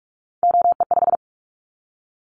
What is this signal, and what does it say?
Morse code, heard as a keyed tone, about 44 wpm, 700 Hz. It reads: OE5
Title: Callsign Prefixes for Top 25 DXCC - 45wpm